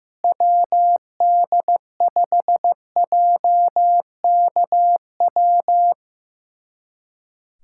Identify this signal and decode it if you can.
Morse code, heard as a keyed tone, about 15 wpm, 700 Hz. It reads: WD5JKW
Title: USA Callsign DE Callsign - 6 Characters - 15wpm